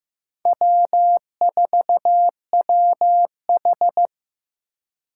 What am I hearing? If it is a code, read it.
Morse code, heard as a keyed tone, about 18 wpm, 700 Hz. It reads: W4WH
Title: Speed Racing - USA 1x2 Call Signs 15wpm